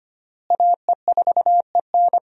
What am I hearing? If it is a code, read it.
Morse code, heard as a keyed tone, about 25 wpm, 700 Hz. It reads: AE4EN
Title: USA Call Signs - 5 Characters - Repeated 3x - 25wpm